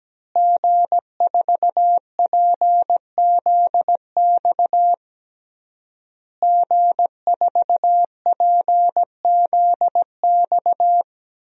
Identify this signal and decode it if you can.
Morse code, heard as a keyed tone, about 17 wpm, 700 Hz. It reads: G4PZX G4PZX
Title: DXCC-100 Call Signs - 5 Characters - Repeated 3x - 17wpm